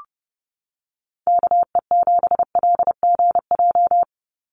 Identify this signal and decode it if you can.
Morse code, heard as a keyed tone, about 30 wpm, 700 Hz. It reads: KE7LGJ
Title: USA Call Signs 30wpm